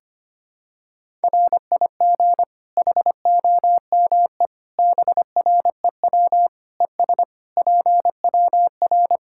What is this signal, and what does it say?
Morse code, heard as a keyed tone, about 25 wpm, 700 Hz. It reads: RIG HOME BREW ES PWR
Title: Practice QSO - Exchange 2 - Info - 25wpm